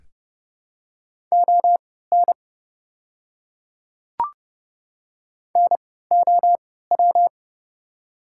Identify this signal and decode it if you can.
Morse code, heard as a keyed tone, about 30 wpm, spaced as for 17 wpm, 700 Hz. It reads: ON NOW
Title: 3x Character Spacing - Top 100 Words - 30wpm